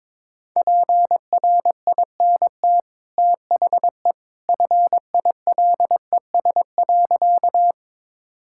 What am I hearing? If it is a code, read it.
Morse code, heard as a keyed tone, about 22 wpm, 700 Hz. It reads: PRINT THE FILES.
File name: practice / 22wpm / Easy Sentences with Hints - 22wpm